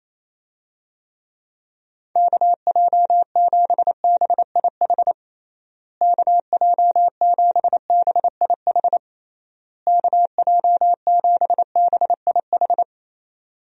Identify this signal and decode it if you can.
Morse code, heard as a keyed tone, about 28 wpm, 700 Hz. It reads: KJ7BIH KJ7BIH KJ7BIH
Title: USA Call Signs - 6 Characters - Repeated 3x - 28wpm